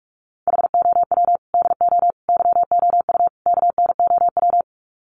Morse code, heard as a keyed tone, about 44 wpm, 700 Hz. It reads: HOW DO YOU KNOW
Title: Sets of 4 Words 45wpm